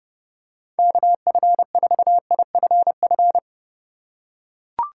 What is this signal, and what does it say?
Morse code, heard as a keyed tone, about 30 wpm, 700 Hz. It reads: KF4IFF